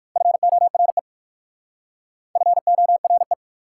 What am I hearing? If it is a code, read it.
Morse code, heard as a keyed tone, about 55 wpm, 700 Hz. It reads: FORE FORE
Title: Speed-Racing Prefixes and Suffixes 45wpm